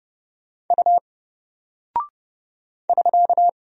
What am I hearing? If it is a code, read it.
Morse code, heard as a keyed tone, about 30 wpm, 700 Hz. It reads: U <SK>
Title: Single Character (Letter:Number:Punctuation:Prosign) Alt 30wpm